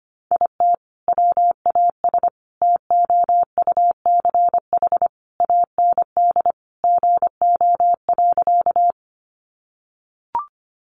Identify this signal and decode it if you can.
Morse code, heard as a keyed tone, about 25 wpm, 700 Hz. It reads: IT WAS TOUCH AND GO.
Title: Sentences from Top 1200 Words 25wpm